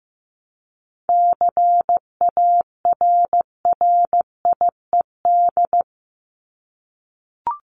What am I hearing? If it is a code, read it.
Morse code, heard as a keyed tone, about 15 wpm, 700 Hz. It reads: CARRIED
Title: ED Suffix - Top 100 Words with Base Words 15wpm